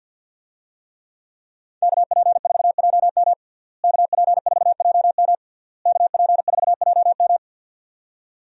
Morse code, heard as a keyed tone, about 51 wpm, 700 Hz. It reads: KP4JM KP4JM KP4JM
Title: DXCC-100 Call Signs - 5 Characters - Repeated 3x - 50wpm